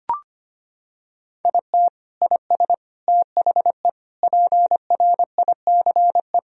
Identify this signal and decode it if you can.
Morse code, heard as a keyed tone, about 25 wpm, 700 Hz. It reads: IT IS THE PRICE
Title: Sentences from Top 1000 Words - Review Part 4 - 25wpm